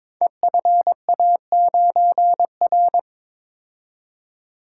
Morse code, heard as a keyed tone, about 22 wpm, 700 Hz. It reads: EFA9R
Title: Code Groups - 5 Letter:Numbers - 22wpm